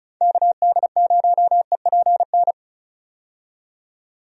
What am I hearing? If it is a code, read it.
Morse code, heard as a keyed tone, about 36 wpm, 700 Hz. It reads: KD0EPN
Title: USA Call Signs - 6 Characters - 35wpm